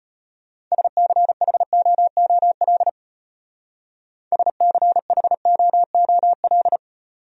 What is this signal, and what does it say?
Morse code, heard as a keyed tone, about 31 wpm, 700 Hz. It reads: SCHOOL SCHOOL
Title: Speed-Racing Top 500 Words 25wpm